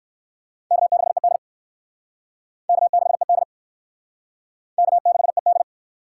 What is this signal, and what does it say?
Morse code, heard as a keyed tone, about 64 wpm, 700 Hz. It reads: K6ED K6ED K6ED